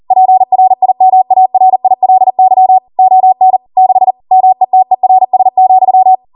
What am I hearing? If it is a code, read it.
Morse code, heard as a keyed tone, about 40 wpm, 750 Hz. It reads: PRIMARILY ON 6 METERS,